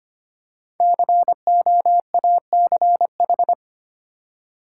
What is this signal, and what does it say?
Morse code, heard as a keyed tone, about 25 wpm, 700 Hz. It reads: COACH
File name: 5 Letter Words - First 3 Letters Given - 25wpm